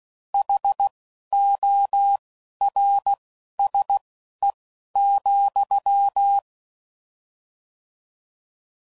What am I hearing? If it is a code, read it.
Morse code, heard as a keyed tone, about 16 wpm, spaced as for 11 wpm, 800 Hz. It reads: HORSE,